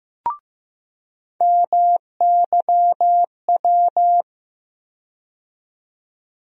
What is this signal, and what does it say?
Morse code, heard as a keyed tone, about 15 wpm, 700 Hz. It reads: MYW
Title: USA Call Sign Postfixes - 3 Letters 15wpm